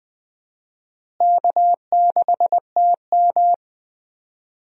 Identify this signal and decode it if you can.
Morse code, heard as a keyed tone, about 20 wpm, 700 Hz. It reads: K6TM
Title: USA Call Signs - 4 Characters - Repeated 2x - 20wpm